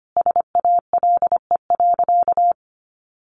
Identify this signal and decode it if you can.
Morse code, heard as a keyed tone, about 25 wpm, 700 Hz. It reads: SALE.
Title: Sentences from Top 1400 Words 25wpm